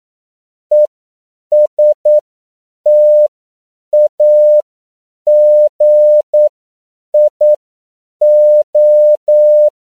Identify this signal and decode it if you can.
Morse code, heard as a keyed tone, about 9 wpm, spaced as for 7 wpm, 600 Hz. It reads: ESTAGIO